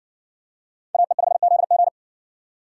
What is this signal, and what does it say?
Morse code, heard as a keyed tone, about 60 wpm, 700 Hz. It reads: AE5ZG